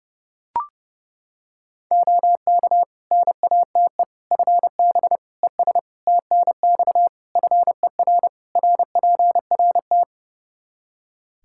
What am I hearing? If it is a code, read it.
Morse code, heard as a keyed tone, about 30 wpm, 700 Hz. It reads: OK NATE FB ES TNX FER RPRT
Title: Practice QSO - Exchange 2 - Info - 30wpm